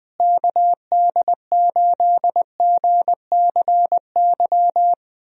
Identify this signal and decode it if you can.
Morse code, heard as a keyed tone, about 20 wpm, 700 Hz. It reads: KD8GCY
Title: USA Call Signs - 6 Characters - Repeated 3x - 20wpm